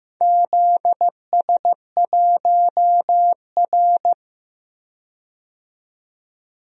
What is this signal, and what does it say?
Morse code, heard as a keyed tone, about 15 wpm, 700 Hz. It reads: ZS1R
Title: DXCC-100 Call Signs - 4 Characters - 15wpm